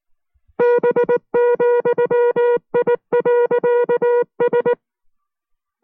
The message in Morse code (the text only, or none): B,I.S